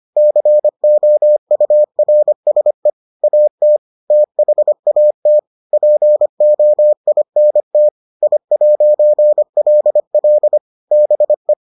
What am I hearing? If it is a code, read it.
Morse code, heard as a keyed tone, about 25 wpm, 600 Hz. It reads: COURSE AT THAT POINT I'LL BE